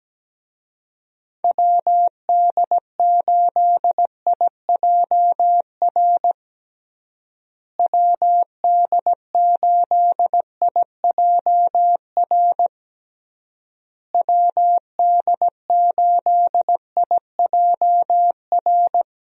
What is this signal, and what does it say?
Morse code, heard as a keyed tone, about 17 wpm, 700 Hz. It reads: WD8IJR WD8IJR WD8IJR
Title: USA Call Signs - 6 Characters - Repeated 3x - 17wpm